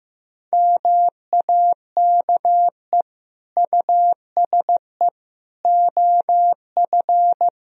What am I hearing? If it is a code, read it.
Morse code, heard as a keyed tone, about 15 wpm, 700 Hz. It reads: MAKE USE OF
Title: Sentences from Top 1000 Words - Review Part 4 - 15wpm